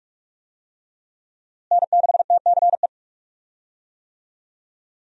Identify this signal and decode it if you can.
Morse code, heard as a keyed tone, about 46 wpm, 700 Hz. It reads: N6TCE
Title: USA Call Signs - 5 Characters - Repeated 3x - 45wpm